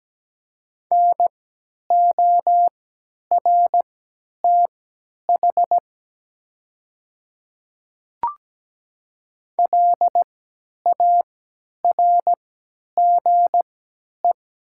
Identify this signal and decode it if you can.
Morse code, heard as a keyed tone, about 17 wpm, spaced as for 10 wpm, 700 Hz. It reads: NORTH LARGE
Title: 3x Character Spacing - Top 300 Words - 17wpm